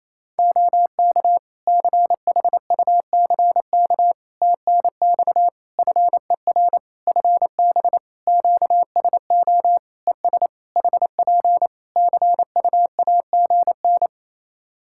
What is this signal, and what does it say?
Morse code, heard as a keyed tone, about 27 wpm, 700 Hz. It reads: OK CHUCK TNX FER FB QSO ES HP CUAGN
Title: Practice QSO - Exchange 4 - Ending - 28wpm